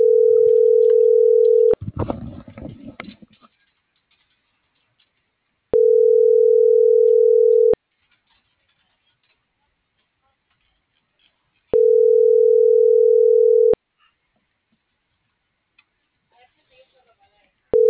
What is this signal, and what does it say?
Ringback tone (North America), 440 + 480 Hz.